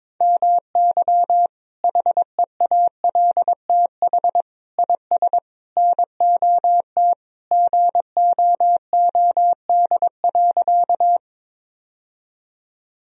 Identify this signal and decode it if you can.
Morse code, heard as a keyed tone, about 22 wpm, 700 Hz. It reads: MY HEALTH IS NOT GOOD.
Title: Sentences from Top 1000 Words - Review Part 3 - 22wpm